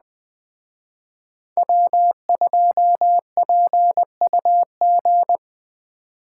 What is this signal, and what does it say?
Morse code, heard as a keyed tone, about 20 wpm, 700 Hz. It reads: W2PUG
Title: Speed Racing - USA Call Signs 17wpm